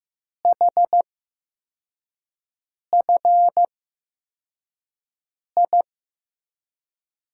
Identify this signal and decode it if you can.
Morse code, heard as a keyed tone, about 15 wpm, spaced as for 4 wpm, 700 Hz. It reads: HFI
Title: USA Call Signs - 6 Characters - 8x Character Spacing - 15wpm